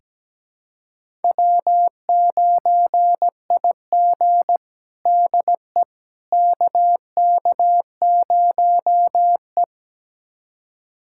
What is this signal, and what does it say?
Morse code, heard as a keyed tone, about 17 wpm, 700 Hz. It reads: W9IG DE KK0E
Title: USA Callsign DE Callsign - 4 Characters - 17wpm